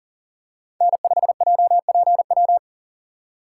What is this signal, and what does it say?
Morse code, heard as a keyed tone, about 40 wpm, 700 Hz. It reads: N5JPW
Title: USA Call Signs - 3x Character Spacing - 40wpm